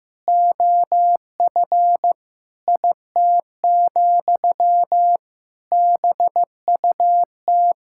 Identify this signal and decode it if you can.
Morse code, heard as a keyed tone, about 15 wpm, 700 Hz. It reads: OF IT, BUT
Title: Sentences from Top 1000 Words - Review Part 6 - 15wpm